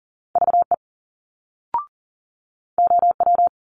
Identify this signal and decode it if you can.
Morse code, heard as a keyed tone, about 40 wpm, 700 Hz. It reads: VE OW
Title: Multi-letter Phonemes Quick Fire 40wpm